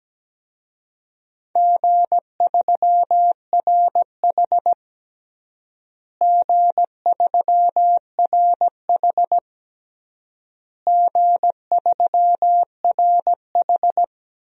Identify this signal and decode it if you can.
Morse code, heard as a keyed tone, about 17 wpm, 700 Hz. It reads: G3RH G3RH G3RH
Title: DXCC-100 Call Signs - 4 Characters - Repeated 3x - 17wpm